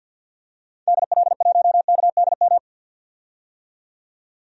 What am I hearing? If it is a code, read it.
Morse code, heard as a keyed tone, about 50 wpm, 700 Hz. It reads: DL1KDM
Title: DXCC-100 Call Signs - 6 Characters - 50wpm